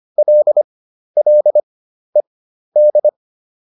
Morse code, heard as a keyed tone, about 25 wpm, spaced as for 12 wpm, 600 Hz. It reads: LLED